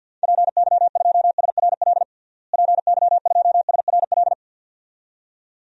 Morse code, heard as a keyed tone, about 48 wpm, 700 Hz. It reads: PY2SRL PY2SRL
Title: DXCC-100 Call Signs - 6 Characters - Repeated 2x - 50wpm